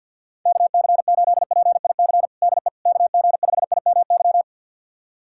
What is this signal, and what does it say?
Morse code, heard as a keyed tone, about 52 wpm, 700 Hz. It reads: KC8PIC DE KG5IMY
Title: USA Callsign DE Callsign - 6 Characters - 50wpm